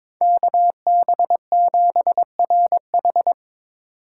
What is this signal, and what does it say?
Morse code, heard as a keyed tone, about 22 wpm, 700 Hz. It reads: KB7RH
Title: USA Call Signs - 2x Character Spacing - 22wpm